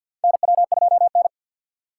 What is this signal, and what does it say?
Morse code, heard as a keyed tone, about 51 wpm, 690 Hz. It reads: NW2N
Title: USA Call Signs - 2x Character Spacing - 50wpm